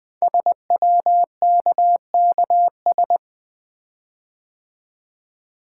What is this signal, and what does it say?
Morse code, heard as a keyed tone, about 20 wpm, 700 Hz. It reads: SWKKS
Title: Code Groups - 5 Letters - 20wpm